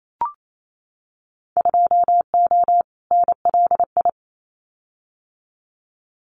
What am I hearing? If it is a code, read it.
Morse code, heard as a keyed tone, about 28 wpm, 700 Hz. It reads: 2O NLI
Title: Winter Field Day Exchanges 28wpm